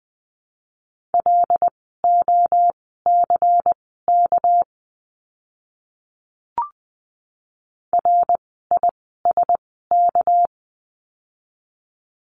Morse code, heard as a keyed tone, about 20 wpm, spaced as for 14 wpm, 700 Hz. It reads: LOCK RISK
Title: 199 Lesson 32f - Letter k - Words 20wpm